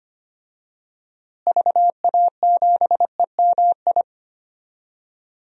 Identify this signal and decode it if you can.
Morse code, heard as a keyed tone, about 25 wpm, 700 Hz. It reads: VA7EMI
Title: DXCC-100 Call Signs - 6 Characters - Repeated 2x - 25wpm